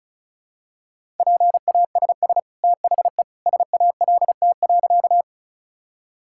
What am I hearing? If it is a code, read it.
Morse code, heard as a keyed tone, about 36 wpm, 700 Hz. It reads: PASS THE SALT.
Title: Sentences from Top 1000 Words - Review Part 5 - 35wpm